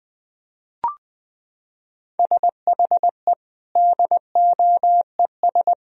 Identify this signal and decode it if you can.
Morse code, heard as a keyed tone, about 21 wpm, 700 Hz. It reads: SHE DOES
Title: Sentences from Top 1000 Words - Review Part 1 - 20wpm